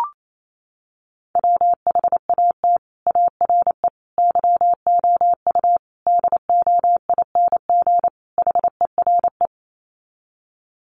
Morse code, heard as a keyed tone, about 28 wpm, 700 Hz. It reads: WHAT ARE YOU DOING HERE
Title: Sets of 5 Words 28wpm